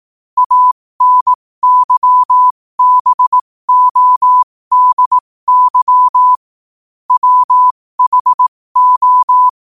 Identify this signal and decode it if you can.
Morse code, heard as a keyed tone, about 18 wpm, spaced as for 15 wpm, 1000 Hz. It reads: ANYBODY WHO
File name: qotdfiles / 15wpm